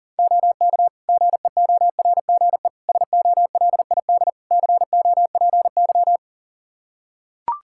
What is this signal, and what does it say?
Morse code, heard as a keyed tone, about 40 wpm, 700 Hz. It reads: OK GEORGE SOLID COPY
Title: K1USN SST QSO Practice 40wpm